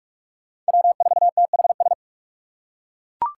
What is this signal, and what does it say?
Morse code, heard as a keyed tone, about 41 wpm, 700 Hz. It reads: W4THS